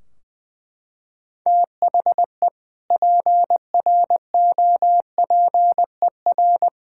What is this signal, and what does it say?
Morse code, heard as a keyed tone, about 21 wpm, 700 Hz. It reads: THE PROPER
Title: Sentences from Top 1600 Words 20wpm